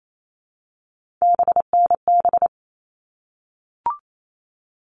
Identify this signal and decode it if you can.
Morse code, heard as a keyed tone, about 28 wpm, 700 Hz. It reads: BNB
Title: All Cut Numbers - 3-Digit 28wpm